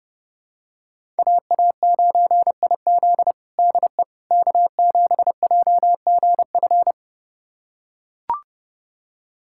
Morse code, heard as a keyed tone, about 30 wpm, 700 Hz. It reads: AA9IZ DE K7JGF
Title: USA Callsign DE Callsign - 5 Characters - 30wpm